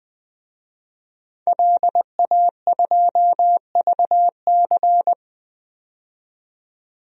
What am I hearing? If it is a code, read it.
Morse code, heard as a keyed tone, about 20 wpm, 700 Hz. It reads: LA2VC
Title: DXCC-100 Call Signs - 5 Characters - 20wpm